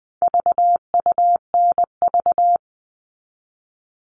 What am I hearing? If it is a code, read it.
Morse code, heard as a keyed tone, about 20 wpm, 700 Hz. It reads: VUNV
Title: All Cut Numbers - 4-Digit 20wpm